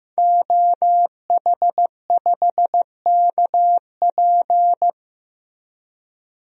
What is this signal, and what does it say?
Morse code, heard as a keyed tone, about 15 wpm, 700 Hz. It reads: OH5KP
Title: DXCC-100 Call Signs - 5 Characters - Repeated 2x - 15wpm